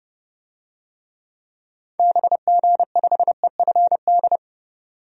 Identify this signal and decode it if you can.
Morse code, heard as a keyed tone, about 32 wpm, 700 Hz. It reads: BG5EFD
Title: DXCC-100 Call Signs - 6 Characters - 30wpm